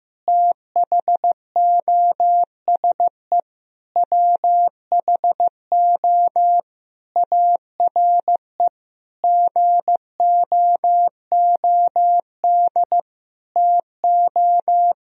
Morse code, heard as a keyed tone, about 15 wpm, 700 Hz. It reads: THOSE WHO ARE GOOD TO
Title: Sentences from Top 1000 Words - Review Part 1 - 15wpm